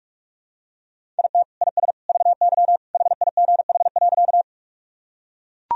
Morse code, heard as a keyed tone, about 42 wpm, 700 Hz. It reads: IT IS VY HIGH.
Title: Sentences from Top 200 Words and Abbreviations 45wpm